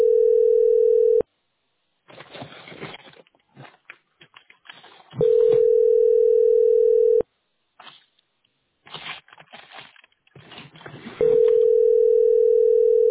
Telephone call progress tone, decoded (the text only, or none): ringback tone